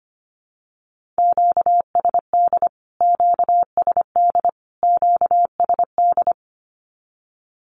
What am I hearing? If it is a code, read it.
Morse code, heard as a keyed tone, about 25 wpm, 700 Hz. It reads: QSD QSD QSD
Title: Q-Codes Repeated 3x 25wpm